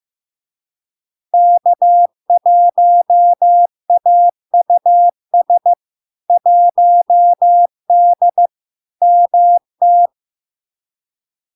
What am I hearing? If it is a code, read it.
Morse code, heard as a keyed tone, about 15 wpm, 700 Hz. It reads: K1AUS 1D MT
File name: LARK-FD-2022-Log-in-CW-15wpm-700-Hz